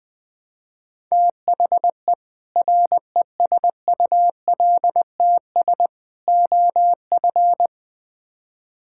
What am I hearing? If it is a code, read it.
Morse code, heard as a keyed tone, about 20 wpm, 700 Hz. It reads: THE RESULTS OF
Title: Build A Phrase - 4-Word Phrases - 20wpm